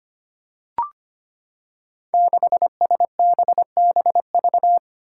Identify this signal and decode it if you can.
Morse code, heard as a keyed tone, about 25 wpm, 700 Hz. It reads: 6SBBV
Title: Five Challenging Letter-Number Combinations 25wpm